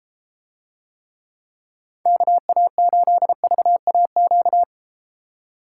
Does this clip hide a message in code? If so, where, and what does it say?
Yes: Morse code from 2.1 to 4.6 s, heard as a keyed tone, about 32 wpm, 700 Hz. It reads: KA8VAQ